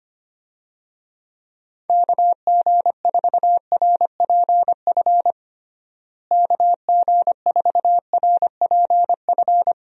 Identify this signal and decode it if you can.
Morse code, heard as a keyed tone, about 25 wpm, 700 Hz. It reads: KG4RPF KG4RPF
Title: USA Call Signs - 6 Characters - Repeated 3x - 25wpm